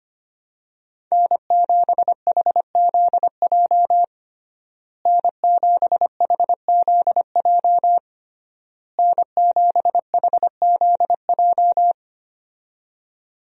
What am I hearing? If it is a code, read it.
Morse code, heard as a keyed tone, about 25 wpm, 700 Hz. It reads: N7HZJ N7HZJ N7HZJ
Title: USA Call Signs - 5 Characters - Repeated 3x - 25wpm